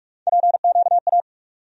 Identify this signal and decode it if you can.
Morse code, heard as a keyed tone, about 42 wpm, 700 Hz. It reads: PQA
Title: USA Call Sign Postfixes - 3 Letters 45wpm